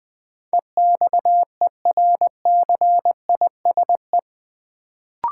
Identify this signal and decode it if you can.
Morse code, heard as a keyed tone, about 20 wpm, 700 Hz. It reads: EXERCISE
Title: Sentences from Top 1000 Words - Review Part 4 - New Word Drill - 20wpm